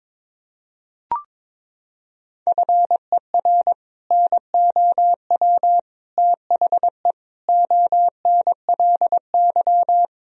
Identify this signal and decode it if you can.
Morse code, heard as a keyed tone, about 21 wpm, 700 Hz. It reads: FER NOW THE ONLY
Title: Sentences from Top 200 Words and Abbreviations 22wpm